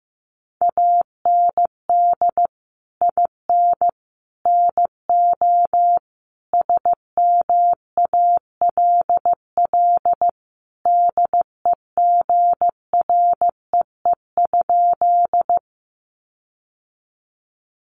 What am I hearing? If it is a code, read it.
Morse code, heard as a keyed tone, about 15 wpm, 700 Hz. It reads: AND IN NO SMALL DEGREE?
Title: Sentences from Top 1500 Words 15wpm